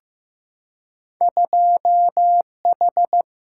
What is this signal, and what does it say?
Morse code, heard as a keyed tone, about 15 wpm, 700 Hz. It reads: 2 H
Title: Single-Letter Single-Number 15wpm